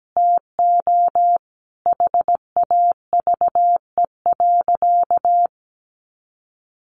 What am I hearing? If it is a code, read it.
Morse code, heard as a keyed tone, about 17 wpm, 700 Hz. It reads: TO HAVE.
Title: Sentences from Top 1400 Words 17wpm